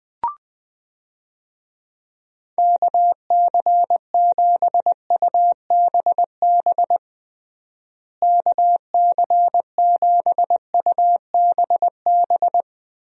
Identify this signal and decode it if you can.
Morse code, heard as a keyed tone, about 20 wpm, 700 Hz. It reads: KC7UBB KC7UBB
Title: USA Call Signs - 6 Characters - Repeated 2x - 20wpm